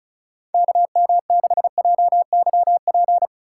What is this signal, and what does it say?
Morse code, heard as a keyed tone, about 34 wpm, 700 Hz. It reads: KM6JYP
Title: USA Call Signs - 6 Characters - 2x Character Spacing - 35wpm